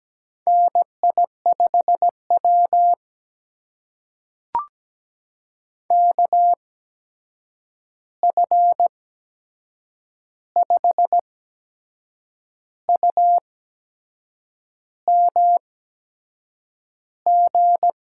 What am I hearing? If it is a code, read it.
Morse code, heard as a keyed tone, about 17 wpm, spaced as for 6 wpm, 700 Hz. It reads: NI5W KF5UMG
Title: USA Call Signs - 8x Character Spacing - 17wpm